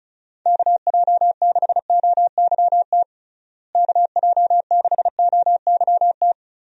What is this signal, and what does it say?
Morse code, heard as a keyed tone, about 36 wpm, 700 Hz. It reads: KJ6OYT KJ6OYT
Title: USA Call Signs - 6 Characters - Repeated 2x - 35wpm